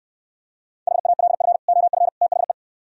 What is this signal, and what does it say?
Morse code, heard as a keyed tone, about 66 wpm, 700 Hz. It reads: HALF OF THE